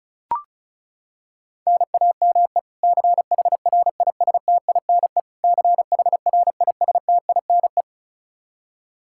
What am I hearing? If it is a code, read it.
Morse code, heard as a keyed tone, about 36 wpm, 700 Hz. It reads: NAME CHRISTINE CHRISTINE
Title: Practice QSO - Exchange 1 - Beginning - 35wpm